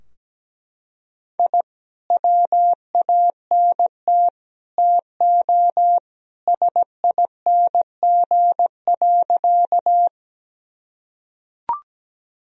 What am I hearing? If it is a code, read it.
Morse code, heard as a keyed tone, about 17 wpm, 700 Hz. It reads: I WANT TO SING.
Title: Sentences from Top 1000 Words - Review Part 2 - 17wpm